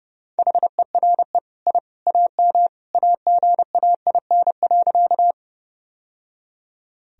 Morse code, heard as a keyed tone, about 32 wpm, 700 Hz. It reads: HERE I AM AGAIN.
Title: Sentences from Top 1000 Words - Review Part 2 - 30wpm